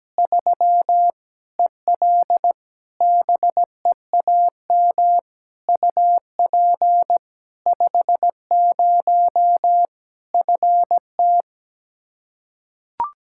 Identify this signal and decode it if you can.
Morse code, heard as a keyed tone, about 17 wpm, 700 Hz. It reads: 3 EL BEAM UP 50 FT
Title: Practice QSO - Exchange 2 - Info - 17wpm